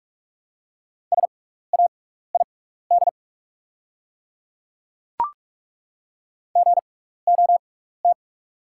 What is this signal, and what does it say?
Morse code, heard as a keyed tone, about 43 wpm, spaced as for 15 wpm, 700 Hz. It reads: SAID GOT
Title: 6x Character Spacing - Top 300 Words - 45wpm